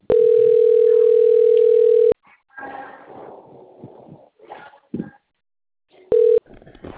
Ringback tone (North America), 440 + 480 Hz.